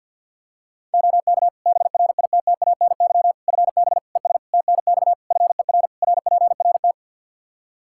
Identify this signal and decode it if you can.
Morse code, heard as a keyed tone, about 46 wpm, 700 Hz. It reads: OK BRITTANY FB ES TNX FER RPRT
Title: Practice QSO - Exchange 2 - Info - 50wpm